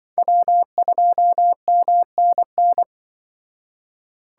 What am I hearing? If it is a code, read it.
Morse code, heard as a keyed tone, about 25 wpm, 700 Hz. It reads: W2MNN
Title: Speed Racing - USA Call Signs 20wpm